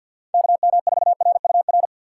Morse code, heard as a keyed tone, about 53 wpm, 700 Hz. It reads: KM4RUR
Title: USA Call Signs - 6 Characters - 8x Character Spacing - 50wpm